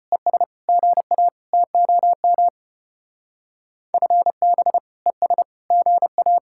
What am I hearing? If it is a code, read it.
Morse code, heard as a keyed tone, about 33 wpm, 700 Hz. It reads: ES GA TOM FB ES GA